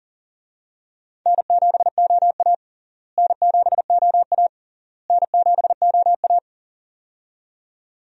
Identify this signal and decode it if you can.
Morse code, heard as a keyed tone, about 40 wpm, 700 Hz. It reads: N7OA N7OA N7OA
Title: USA Call Signs - 4 Characters - Repeated 3x - 40wpm